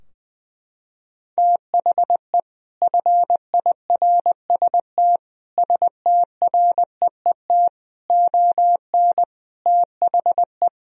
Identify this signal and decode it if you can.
Morse code, heard as a keyed tone, about 20 wpm, 700 Hz. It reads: THE FIRST STREET ON THE
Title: Sentences from Top 1000 Words - Review Part 3 - 20wpm